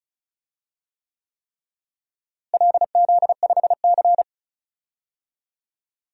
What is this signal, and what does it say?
Morse code, heard as a keyed tone, about 36 wpm, 700 Hz. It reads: LZ5C
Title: DXCC-100 Call Signs - 4 Characters - Repeated 3x - 35wpm